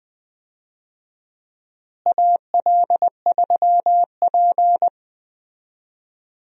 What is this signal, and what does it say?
Morse code, heard as a keyed tone, about 20 wpm, 700 Hz. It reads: AL3P
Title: USA Call Signs - 4 Characters - Repeated 3x - 20wpm